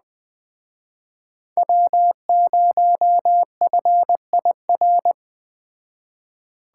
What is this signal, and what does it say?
Morse code, heard as a keyed tone, about 20 wpm, 700 Hz. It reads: W0FIR
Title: Speed Racing - USA 1x3 Call Signs 17wpm